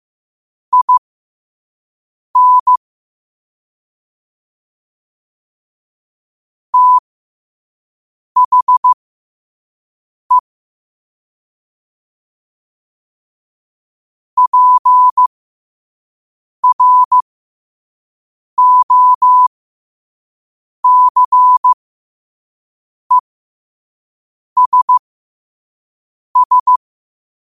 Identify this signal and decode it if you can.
Morse code, heard as a keyed tone, about 15 wpm, spaced as for 5 wpm, 1000 Hz. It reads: IN THE PROCESS